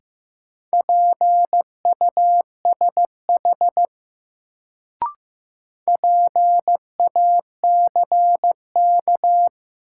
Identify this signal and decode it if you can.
Morse code, heard as a keyed tone, about 15 wpm, 700 Hz. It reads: PUSH PACK
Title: Sentences from Top 1300 Words - New Word Drill - 15wpm